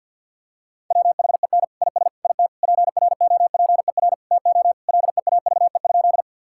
Morse code, heard as a keyed tone, about 52 wpm, 700 Hz. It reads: WHEN IS IT PROPER TO LEAVE?
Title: Sentences from Top 1000 Words - Review Part 5 - 50wpm